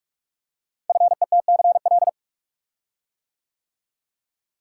Morse code, heard as a keyed tone, about 44 wpm, 700 Hz. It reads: FETCL
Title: Code Groups - 5 Letters - 45wpm